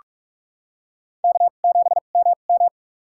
Morse code, heard as a keyed tone, about 44 wpm, spaced as for 32 wpm, 690 Hz. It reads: K7MM
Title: USA Call Signs - 4 Characters - 2x Character Spacing - 45wpm